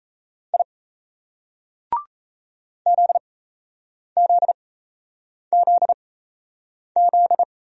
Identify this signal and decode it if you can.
Morse code, heard as a keyed tone, about 35 wpm, spaced as for 8 wpm, 700 Hz. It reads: I ZZZZ
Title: Speed-Racing Single Letters 28wpm